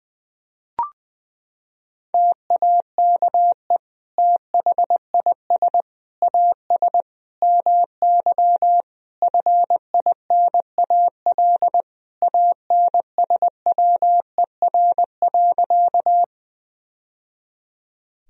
Morse code, heard as a keyed tone, about 20 wpm, 700 Hz. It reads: TAKE THIS AS MY FINAL ANSWER.
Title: Sentences from Top 1000 Words - Review Part 2 - 20wpm